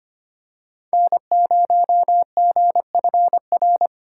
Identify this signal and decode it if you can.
Morse code, heard as a keyed tone, about 25 wpm, 700 Hz. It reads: N0GFR
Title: USA Call Signs - 5 Characters - 4x Character Spacing - 25wpm